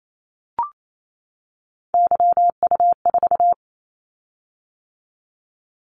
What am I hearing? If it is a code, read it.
Morse code, heard as a keyed tone, about 28 wpm, 700 Hz. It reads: YU4
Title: Callsign Prefixes for Top 25 DXCC - 28wpm